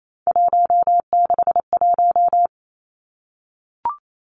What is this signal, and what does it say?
Morse code, heard as a keyed tone, about 28 wpm, 700 Hz. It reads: 161
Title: Three Challenging Letter-Number Combinations 28wpm